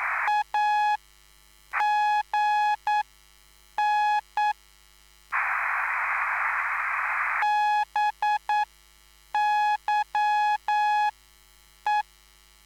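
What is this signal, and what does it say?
Morse code, heard as a keyed tone, about 9 wpm, spaced as for 7 wpm, 840 Hz. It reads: AGN BYE
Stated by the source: Live CW QSO between PA2BX and SP3JFK on the 14055 kHz QRS frequency, at a basic 8 wpm.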